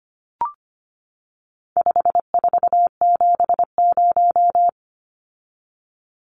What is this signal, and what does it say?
Morse code, heard as a keyed tone, about 25 wpm, 700 Hz. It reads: 5470
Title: Numbers - 4-Digit 25wpm